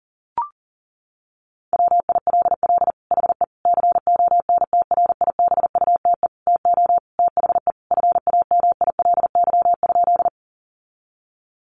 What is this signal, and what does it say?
Morse code, heard as a keyed tone, about 40 wpm, 700 Hz. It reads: WILL HE CONTRIBUTE TO THE FAMILY?
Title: Sentences from Top 1400 Words 40wpm